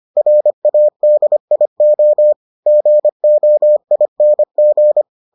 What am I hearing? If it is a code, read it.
Morse code, heard as a keyed tone, about 25 wpm, 600 Hz. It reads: RADIO GOING